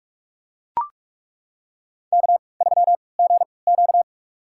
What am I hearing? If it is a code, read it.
Morse code, heard as a keyed tone, about 42 wpm, spaced as for 25 wpm, 700 Hz. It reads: K3GQ
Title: USA Call Signs - 3x Character Spacing - 45wpm